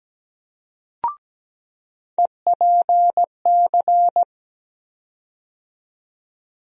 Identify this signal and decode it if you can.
Morse code, heard as a keyed tone, about 17 wpm, 700 Hz. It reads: EPC
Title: USA Call Sign Postfixes - 3 Letters 17wpm